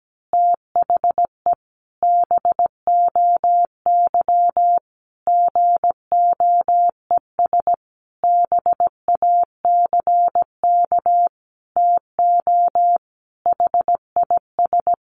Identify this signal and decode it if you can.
Morse code, heard as a keyed tone, about 17 wpm, 700 Hz. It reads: THE BOY GOES BACK TO HIS
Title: Easy Phrases 17wpm